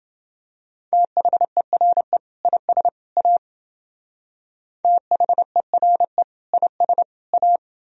Morse code, heard as a keyed tone, about 30 wpm, 700 Hz. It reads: THERE IS A THERE IS A